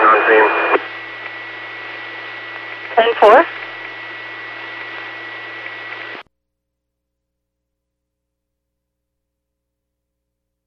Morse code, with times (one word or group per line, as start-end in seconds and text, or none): none